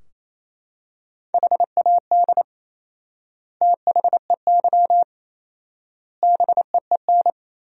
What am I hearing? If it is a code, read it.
Morse code, heard as a keyed tone, about 28 wpm, 700 Hz. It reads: HAD THEY BEEN
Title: 4x Word-Spacing - Sentences from Top 400 Words - 28wpm